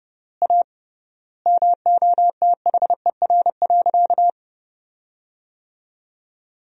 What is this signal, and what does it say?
Morse code, heard as a keyed tone, about 31 wpm, 700 Hz. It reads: A MOTHER.
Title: 3x Word-Spacing - Sentences from Top 200 Words - 30wpm